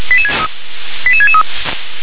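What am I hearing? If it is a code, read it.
Five-tone selective call (ZVEI), heard as quick runs of five tones, tones of about 70 ms. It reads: 88663
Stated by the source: Five-tone ZVEI selective call received by SDR.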